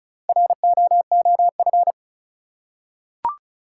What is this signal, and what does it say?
Morse code, heard as a keyed tone, about 36 wpm, 700 Hz. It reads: ROOF
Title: Four Letter Words - 35wpm